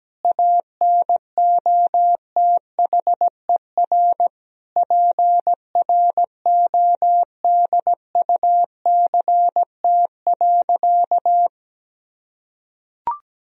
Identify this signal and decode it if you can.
Morse code, heard as a keyed tone, about 17 wpm, 700 Hz. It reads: ANOTHER PRODUCT.
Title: Sentences from Top 1000 Words - Review Part 3 - 17wpm